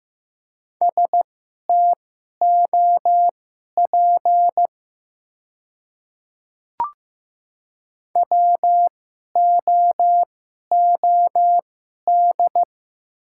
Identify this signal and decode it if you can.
Morse code, heard as a keyed tone, about 15 wpm, spaced as for 11 wpm, 700 Hz. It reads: STOP WOOD
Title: 2x Character Spacing - Top 300 Words - 15wpm